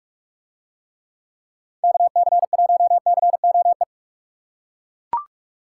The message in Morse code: KC1COE

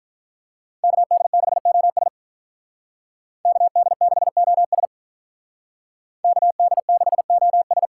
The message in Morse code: KD6OS KD6OS KD6OS

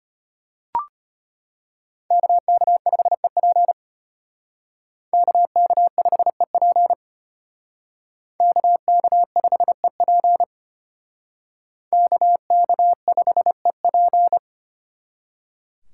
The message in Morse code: KK5EP KK5EP KK5EP KK5EP

